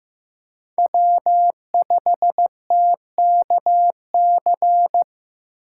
Morse code, heard as a keyed tone, about 16 wpm, 700 Hz. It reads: W5TKC
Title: USA Call Signs - 5 Characters - 3x Character Spacing - 15wpm